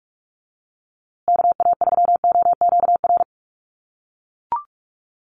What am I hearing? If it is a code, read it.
Morse code, heard as a keyed tone, about 44 wpm, 700 Hz. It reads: KA3OQR
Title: USA Call Signs Repeated 3x 45wpm